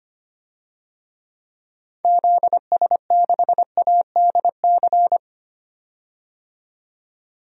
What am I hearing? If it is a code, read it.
Morse code, heard as a keyed tone, about 25 wpm, 700 Hz. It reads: ZS6ADC
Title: DXCC-100 Call Signs - 6 Characters - 25wpm